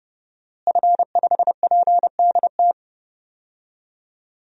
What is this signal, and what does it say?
Morse code, heard as a keyed tone, about 30 wpm, 700 Hz. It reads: F5PDT
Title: DXCC-100 Call Signs - 5 Characters - Repeated 2x - 30wpm